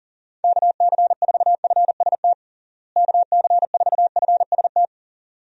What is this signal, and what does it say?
Morse code, heard as a keyed tone, about 40 wpm, 700 Hz. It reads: KC4FST KC4FST
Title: USA Call Signs - 6 Characters - Repeated 2x - 40wpm